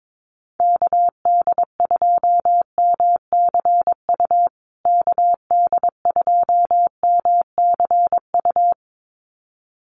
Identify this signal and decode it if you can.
Morse code, heard as a keyed tone, about 22 wpm, 700 Hz. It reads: KD2MCU KD2MCU
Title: USA Call Signs Repeated 2x 22wpm